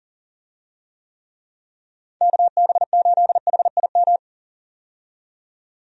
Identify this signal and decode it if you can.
Morse code, heard as a keyed tone, about 40 wpm, 700 Hz. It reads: KB8HIM
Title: USA Call Signs - 6 Characters - 40wpm